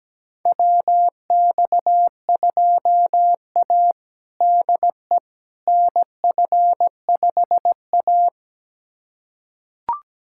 Morse code, heard as a keyed tone, about 17 wpm, 700 Hz. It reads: WX2A DE NF5A
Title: USA Callsign DE Callsign - 4 Characters - 17wpm